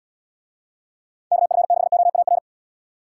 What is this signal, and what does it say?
Morse code, heard as a keyed tone, about 78 wpm, 700 Hz. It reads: KF6OAR